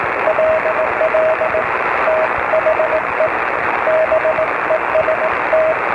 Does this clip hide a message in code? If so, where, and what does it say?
Yes: Morse code from 0.3 to 5.7 s, heard as a keyed tone, about 18 wpm, 630 Hz, with radio noise. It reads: LL THE BEST